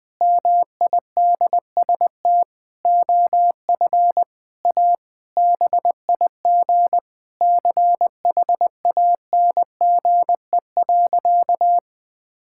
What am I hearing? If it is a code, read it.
Morse code, heard as a keyed tone, about 20 wpm, 700 Hz. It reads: MIDST OF A BIG CHANGE.
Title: Sentences with 7-Word Phrases - 20wpm